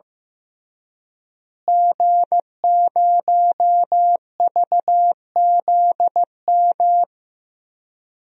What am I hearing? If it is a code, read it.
Morse code, heard as a keyed tone, about 15 wpm, 700 Hz. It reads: G0VZM
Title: DXCC-100 Call Signs - 5 Characters - Repeated 3x - 15wpm